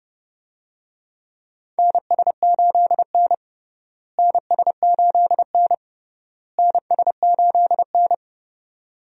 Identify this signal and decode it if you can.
Morse code, heard as a keyed tone, about 30 wpm, 700 Hz. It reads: NS8N NS8N NS8N
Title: USA Call Signs - 4 Characters - Repeated 3x - 30wpm